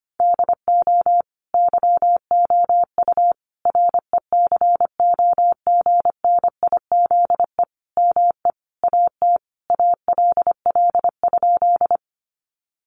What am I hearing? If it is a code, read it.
Morse code, heard as a keyed tone, about 25 wpm, 700 Hz. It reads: DO YOU RECOGNIZE ME AT ALL?
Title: Sentences from Top 1400 Words 25wpm